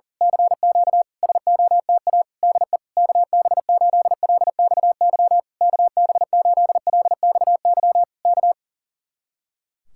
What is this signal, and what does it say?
Morse code, heard as a keyed tone, about 35 wpm, 700 Hz. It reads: CQ SOTA DE KB8LXY KB8LXY K
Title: Speed-Racing Common QSO Phrases - 40wpm